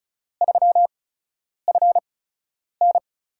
Morse code, heard as a keyed tone, about 34 wpm, spaced as for 10 wpm, 700 Hz. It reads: 3FN
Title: USA Call Signs - 5 Characters - 8x Character Spacing - 35wpm